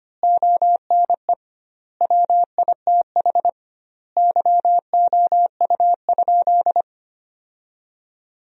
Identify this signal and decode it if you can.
Morse code, heard as a keyed tone, about 25 wpm, 700 Hz. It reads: ONE WITH YOU?
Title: 2x Word-Spacing - Sentences from Top 500 Words - 25wpm